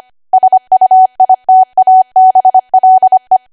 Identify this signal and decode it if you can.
Morse code, heard as a keyed tone, about 25 wpm, 750 Hz. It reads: SUITABLE
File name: cwa / W1AW-Level3 / 25WPM